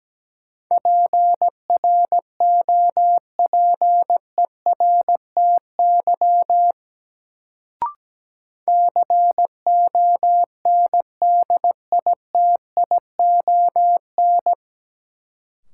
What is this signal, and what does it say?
Morse code, heard as a keyed tone, about 17 wpm, 700 Hz. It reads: PROPERTY CONDITION
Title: Sentences from Top 1000 Words - New Word Drill - 17wpm